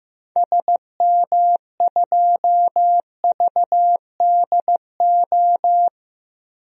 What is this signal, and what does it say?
Morse code, heard as a keyed tone, about 15 wpm, 700 Hz. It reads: SM2VDO
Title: DXCC-100 Call Signs - 6 Characters - 15wpm